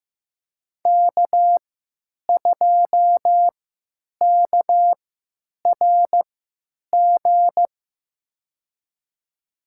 Morse code, heard as a keyed tone, about 15 wpm, spaced as for 9 wpm, 700 Hz. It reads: K2KRG
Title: USA Call Signs - 5 Characters - 3x Character Spacing - 15wpm